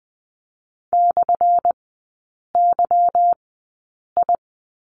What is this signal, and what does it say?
Morse code, heard as a keyed tone, about 20 wpm, spaced as for 8 wpm, 700 Hz. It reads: /YI
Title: 178 Lesson 29e - Slash - Three Characters 20wpm